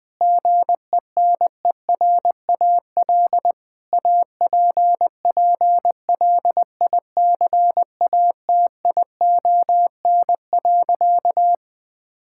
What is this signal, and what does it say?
Morse code, heard as a keyed tone, about 20 wpm, 700 Hz. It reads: GENERAL APPLICATION.
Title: Sentences from Top 1600 Words 20wpm